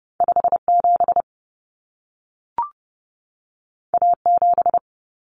Morse code, heard as a keyed tone, about 30 wpm, 700 Hz. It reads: 57 A7
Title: Common Cut Numbers - 2-Digit 30wpm